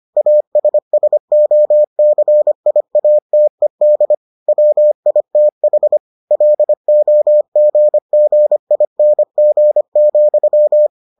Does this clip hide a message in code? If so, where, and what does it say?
Yes: Morse code from 0.2 to 10.9 s, heard as a keyed tone, about 25 wpm, 600 Hz. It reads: ASSOCIATED WITH LOGGING,